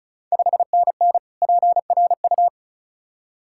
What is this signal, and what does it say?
Morse code, heard as a keyed tone, about 34 wpm, 690 Hz. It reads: 5NN PRU
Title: Ontario QSO Party Exchange 35wpm